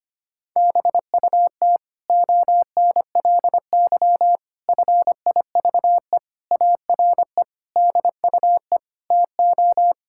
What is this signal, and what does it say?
Morse code, heard as a keyed tone, about 25 wpm, 700 Hz. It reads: BUT ONLY FIVE ARE DUE TO
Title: Sentences from Top 1000 Words - Review Part 6 - 25wpm